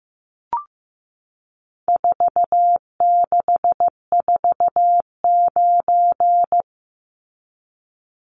Morse code, heard as a keyed tone, about 14 wpm, 700 Hz. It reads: 4649
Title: Numbers - 4-Digit 15wpm